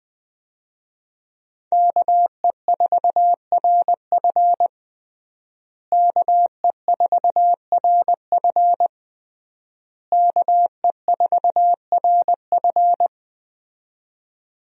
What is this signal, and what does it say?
Morse code, heard as a keyed tone, about 20 wpm, 700 Hz. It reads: KE4RF KE4RF KE4RF
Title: USA Call Signs - 5 Characters - Repeated 3x - 20wpm